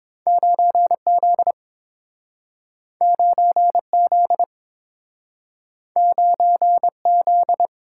Morse code, heard as a keyed tone, about 30 wpm, 700 Hz. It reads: 9Z 9Z 9Z